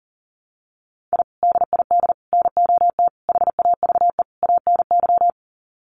Morse code, heard as a keyed tone, about 40 wpm, 700 Hz. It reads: I DID NOT HAVE ANY
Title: Sets of 5 Words Encore 40wpm